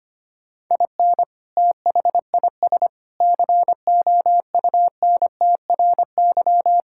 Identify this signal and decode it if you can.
Morse code, heard as a keyed tone, about 25 wpm, 700 Hz. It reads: IN THIS COUNTRY
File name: practice / 25wpm / Sentences with 3-Word Phrases - 25wpm